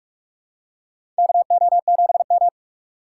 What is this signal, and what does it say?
Morse code, heard as a keyed tone, about 46 wpm, 700 Hz. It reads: KO7M
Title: USA Call Signs - 4 Characters - 6x Character Spacing - 45wpm